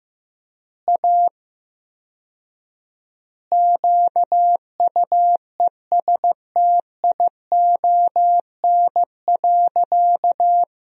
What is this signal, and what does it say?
Morse code, heard as a keyed tone, about 15 wpm, 700 Hz. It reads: A QUESTION.
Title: 4x Word-Spacing - Sentences from Top 400 Words - 15wpm